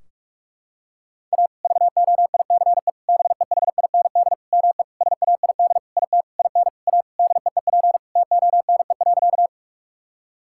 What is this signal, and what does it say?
Morse code, heard as a keyed tone, about 46 wpm, 700 Hz. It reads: A VOICE BEHIND ME SAID IT IN A DEEP TONE.
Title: Sentences from Top 1600 Words 45wpm